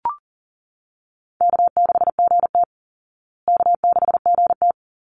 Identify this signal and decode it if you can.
Morse code, heard as a keyed tone, about 40 wpm, 700 Hz. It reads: K6GT K6GT
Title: Call Signs Repeated 3x - 40wpm